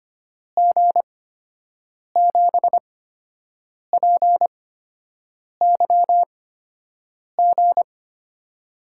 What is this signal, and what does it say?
Morse code, heard as a keyed tone, about 25 wpm, spaced as for 7 wpm, 700 Hz. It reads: G7PYG
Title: USA Call Signs - 6 Characters - 8x Character Spacing - 25wpm